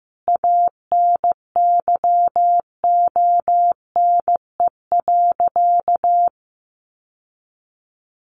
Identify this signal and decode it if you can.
Morse code, heard as a keyed tone, about 15 wpm, 700 Hz. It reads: ANYONE.
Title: Sentences from Top 900 Words 15wpm